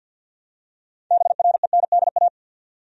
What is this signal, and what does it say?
Morse code, heard as a keyed tone, about 53 wpm, 700 Hz. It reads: BRENDA
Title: USA Common Names - 50wpm